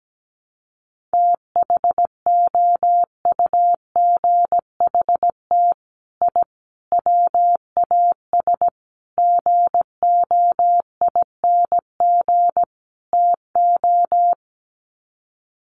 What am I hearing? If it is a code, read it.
Morse code, heard as a keyed tone, about 17 wpm, 700 Hz. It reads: THOUGHT I WAS GOING TO
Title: Sets of 5 Words Encore2 17wpm